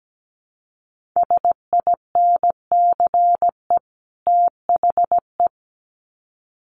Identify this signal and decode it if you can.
Morse code, heard as a keyed tone, about 17 wpm, 700 Hz. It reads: SINCE THE
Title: Sets of 2 Words 17wpm